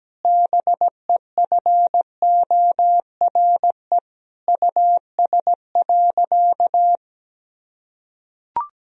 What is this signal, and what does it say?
Morse code, heard as a keyed tone, about 17 wpm, 700 Hz. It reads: BEFORE US.
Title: Sentences from Top 1000 Words - Review Part 4 - 17wpm